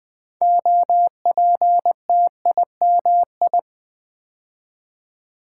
Morse code, heard as a keyed tone, about 21 wpm, 700 Hz. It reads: OPTIMI
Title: Build A Word - 3 Letters At A Time - 8-Letter Words - 20wpm